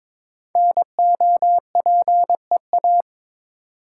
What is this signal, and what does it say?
Morse code, heard as a keyed tone, about 22 wpm, 700 Hz. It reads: NOPEA
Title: Code Groups - 5 Letters - 22wpm